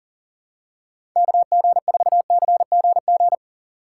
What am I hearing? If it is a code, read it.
Morse code, heard as a keyed tone, about 37 wpm, 700 Hz. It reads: KG4CGG
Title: USA Call Signs - 6 Characters - 6x Character Spacing - 40wpm